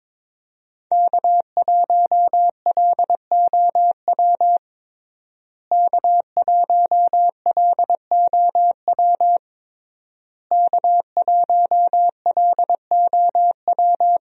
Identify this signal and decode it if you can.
Morse code, heard as a keyed tone, about 22 wpm, 700 Hz. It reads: K1LOW K1LOW K1LOW
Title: USA Call Signs - 5 Characters - Repeated 3x - 22wpm